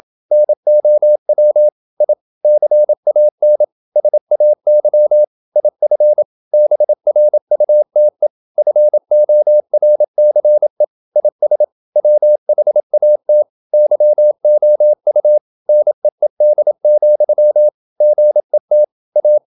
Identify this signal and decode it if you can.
Morse code, heard as a keyed tone, about 27 wpm, 600 Hz. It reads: NOW I CAN SAY IF BRUTE FORCE IS WHAT YOU NEED, GET A
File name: Never 0027wpm